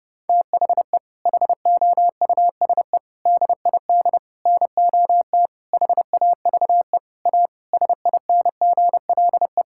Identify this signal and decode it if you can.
Morse code, heard as a keyed tone, about 32 wpm, 700 Hz. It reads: THE HOUSE DID NOT HAVE A SINGLE
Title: Sentences from Top 1600 Words 30wpm